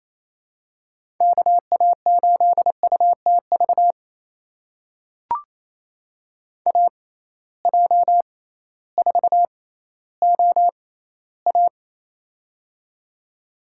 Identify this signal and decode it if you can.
Morse code, heard as a keyed tone, about 28 wpm, spaced as for 15 wpm, 700 Hz. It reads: KA8UTV AJ4OA
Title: USA Call Signs - 6x Character Spacing - 28wpm